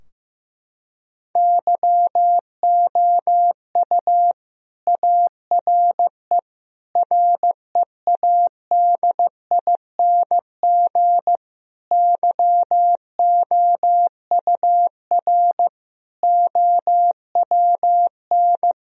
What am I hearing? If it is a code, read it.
Morse code, heard as a keyed tone, about 16 wpm, 700 Hz. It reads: YOU ARE READING YOUR OWN
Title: Sentences from Top 1000 Words - Review Part 3 - 15wpm